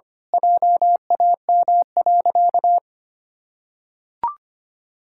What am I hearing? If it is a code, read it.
Morse code, heard as a keyed tone, about 25 wpm, 700 Hz. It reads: JAM.
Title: Pangram Challenge v2 - 25wpm